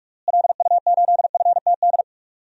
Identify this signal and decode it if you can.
Morse code, heard as a keyed tone, about 46 wpm, 700 Hz. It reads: LU8FTD
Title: DXCC-100 Call Signs - 6 Characters - Repeated 2x - 45wpm